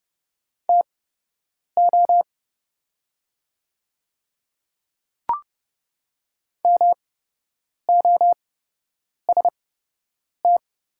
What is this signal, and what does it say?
Morse code, heard as a keyed tone, about 30 wpm, spaced as for 8 wpm, 700 Hz. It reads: TO MOST
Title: 8x Character Spacing - Top 100 Words - 30wpm